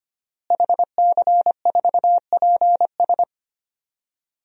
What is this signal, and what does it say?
Morse code, heard as a keyed tone, about 25 wpm, 700 Hz. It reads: HC4PS
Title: DXCC-100 Call Signs - 5 Characters - Repeated 3x - 25wpm